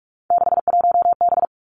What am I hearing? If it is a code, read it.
Morse code, heard as a keyed tone, about 46 wpm, 700 Hz. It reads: 61B
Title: Three Challenging Letter-Number Combinations 45wpm